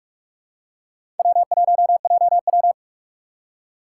Morse code, heard as a keyed tone, about 44 wpm, 700 Hz. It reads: W1JW
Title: USA Call Signs - 3x Character Spacing - 45wpm